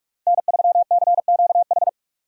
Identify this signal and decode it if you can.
Morse code, heard as a keyed tone, about 44 wpm, 690 Hz. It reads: N3CQH
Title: USA Call Signs - 5 Characters - 6x Character Spacing - 45wpm